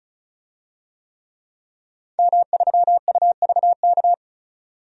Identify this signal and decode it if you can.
Morse code, heard as a keyed tone, about 34 wpm, 700 Hz. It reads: M3UVK
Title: DXCC-100 Call Signs - 5 Characters - 35wpm